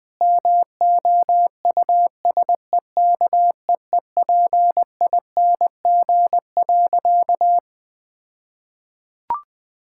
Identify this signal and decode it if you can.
Morse code, heard as a keyed tone, about 20 wpm, 700 Hz. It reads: MOUSEKEEPING.
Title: All-Age Appropriate Jokes - 20wpm